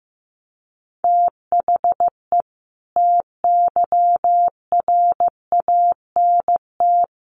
Morse code, heard as a keyed tone, about 15 wpm, 700 Hz. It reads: THE TYRANT
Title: A Sound Of Thunder 15wpm